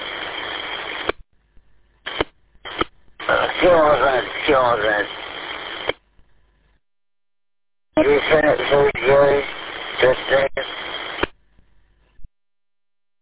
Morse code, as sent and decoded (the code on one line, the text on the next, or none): none